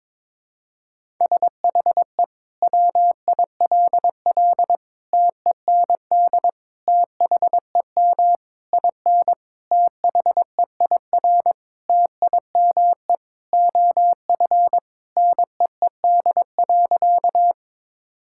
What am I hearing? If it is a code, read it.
Morse code, heard as a keyed tone, about 22 wpm, 700 Hz. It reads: SHE WILL TEND THEM IN THEIR TIME OF NEED.
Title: Sentences from Top 1000 Words - Review Part 5 - 22wpm